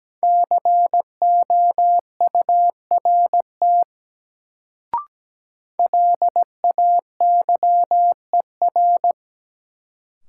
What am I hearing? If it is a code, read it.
Morse code, heard as a keyed tone, about 17 wpm, 700 Hz. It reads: COURT LAYER
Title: Sentences from Top 1600 Words - New Word Drill Alt - 17wpm